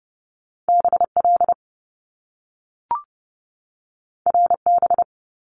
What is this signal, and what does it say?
Morse code, heard as a keyed tone, about 30 wpm, 700 Hz. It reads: BL RB
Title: Top Two Letter Combinations in Words 30wpm